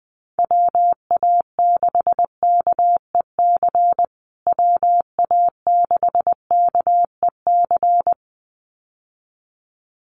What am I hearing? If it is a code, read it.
Morse code, heard as a keyed tone, about 20 wpm, 700 Hz. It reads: WA6KEC WA6KEC
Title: USA Call Signs Repeated 2x 20wpm